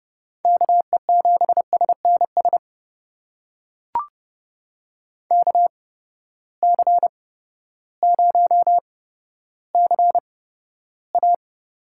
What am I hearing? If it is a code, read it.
Morse code, heard as a keyed tone, about 32 wpm, spaced as for 14 wpm, 700 Hz. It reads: KE7SNS KC0CA
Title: USA Call Signs - 6 Characters - 8x Character Spacing - 30wpm